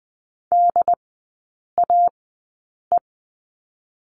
Morse code, heard as a keyed tone, about 20 wpm, spaced as for 8 wpm, 700 Hz. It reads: DAE
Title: 067 Lesson 12e - Letter d - Three Characters 20wpm